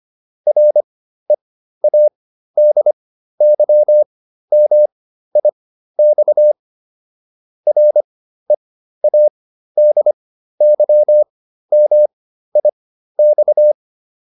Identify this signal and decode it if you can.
Morse code, heard as a keyed tone, about 25 wpm, spaced as for 13 wpm, 600 Hz. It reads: READYMIX READYMIX
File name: RE8 13